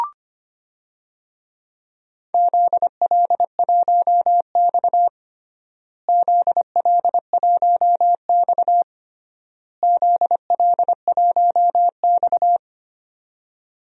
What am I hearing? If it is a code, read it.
Morse code, heard as a keyed tone, about 25 wpm, 700 Hz. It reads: ZL1X ZL1X ZL1X
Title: DXCC-100 Call Signs - 4 Characters - Repeated 3x - 25wpm